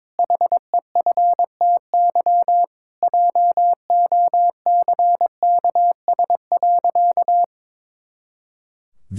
Morse code, heard as a keyed tone, about 22 wpm, 700 Hz. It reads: HEFTY JOCKS.
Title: Pangram Challenge v2 - 22wpm